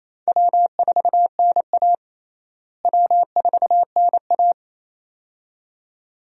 Morse code, heard as a keyed tone, about 27 wpm, 700 Hz. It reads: W4NA W4NA
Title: USA Call Signs - 4 Characters - Repeated 2x - 28wpm